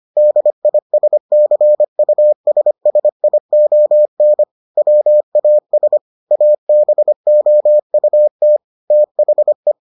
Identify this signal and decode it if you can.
Morse code, heard as a keyed tone, about 25 wpm, 600 Hz. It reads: DISCUSSION WAS ABOUT THE